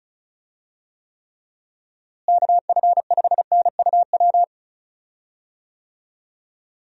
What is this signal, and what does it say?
Morse code, heard as a keyed tone, about 34 wpm, 700 Hz. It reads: KF5NUW
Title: USA Call Signs - 6 Characters - Repeated 2x - 35wpm